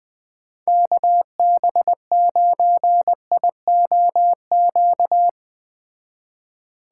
Morse code, heard as a keyed tone, about 21 wpm, 700 Hz. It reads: KB9IOQ